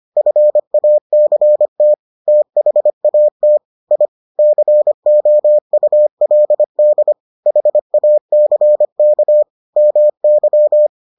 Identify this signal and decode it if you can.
Morse code, heard as a keyed tone, about 25 wpm, 600 Hz. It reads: FACT THAT I COULD HACK MY